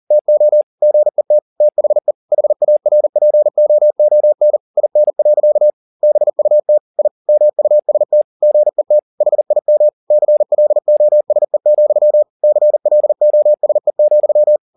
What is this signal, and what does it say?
Morse code, heard as a keyed tone, about 40 wpm, 600 Hz. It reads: TO GET THE HARPOON IN. BUT I MUST GET HIM CLOSE, CLOSE,
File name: the old man and the sea 2 40wpm